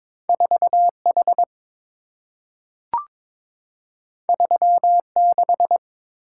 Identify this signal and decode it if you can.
Morse code, heard as a keyed tone, about 23 wpm, 700 Hz. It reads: 4H 36
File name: practice / 22wpm / Two Letter-Number Combinations - 22wpm